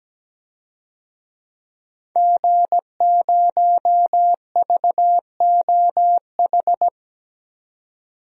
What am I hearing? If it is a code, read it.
Morse code, heard as a keyed tone, about 17 wpm, 700 Hz. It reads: G0VOH
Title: DXCC-100 Call Signs - 5 Characters - 17wpm